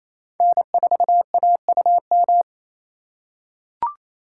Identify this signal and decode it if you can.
Morse code, heard as a keyed tone, about 28 wpm, 700 Hz. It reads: N4AUM
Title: USA Call Signs - 4x Character Spacing - 28wpm